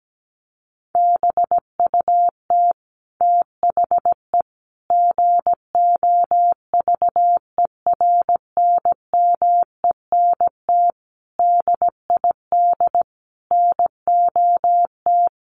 Morse code, heard as a keyed tone, about 17 wpm, 700 Hz. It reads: BUT THE GOVERNMENT DID NOT
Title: Sentences from Top 1400 Words 17wpm